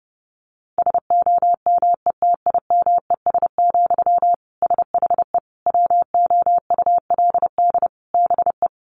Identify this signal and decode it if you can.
Morse code, heard as a keyed tone, about 30 wpm, 700 Hz. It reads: SOMETIMES, SHE WOULD BE
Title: Sentences from Top 1000 Words 30wpm